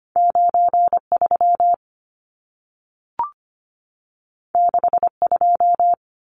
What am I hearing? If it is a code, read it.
Morse code, heard as a keyed tone, about 25 wpm, 700 Hz. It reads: 93 62
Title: Numbers - 2-Digit 25wpm